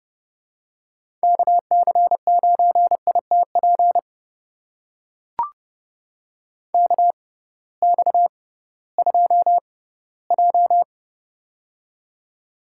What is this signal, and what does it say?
Morse code, heard as a keyed tone, about 32 wpm, spaced as for 18 wpm, 700 Hz. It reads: KC9ITP KX2J
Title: USA Call Signs - 6x Character Spacing - 30wpm